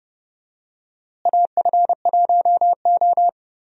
Morse code, heard as a keyed tone, about 32 wpm, 700 Hz. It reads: AF1O
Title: USA Call Signs - 4 Characters - Repeated 2x - 30wpm